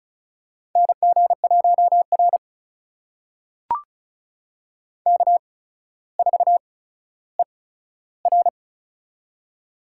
Morse code, heard as a keyed tone, about 36 wpm, spaced as for 15 wpm, 700 Hz. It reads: NG1R K4ER
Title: USA Call Signs - 8x Character Spacing - 35wpm